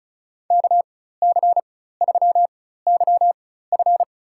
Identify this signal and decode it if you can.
Morse code, heard as a keyed tone, about 34 wpm, spaced as for 16 wpm, 700 Hz. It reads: KC3YF
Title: USA Call Signs - 5 Characters - 4x Character Spacing - 35wpm